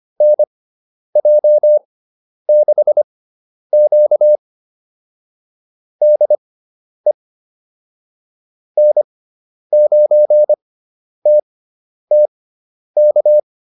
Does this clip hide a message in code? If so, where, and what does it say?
Yes: Morse code from 0.2 to 13.4 s, heard as a keyed tone, about 25 wpm, spaced as for 10 wpm, 600 Hz. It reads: NJ6Q DE N9TTK